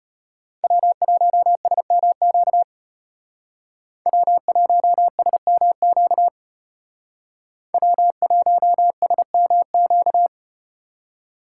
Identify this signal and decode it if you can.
Morse code, heard as a keyed tone, about 32 wpm, 700 Hz. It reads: W1SMQ W1SMQ W1SMQ